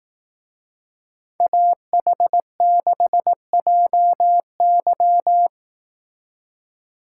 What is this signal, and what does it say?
Morse code, heard as a keyed tone, about 18 wpm, 700 Hz. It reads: AH6JY
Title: Speed Racing - USA Call Signs 15wpm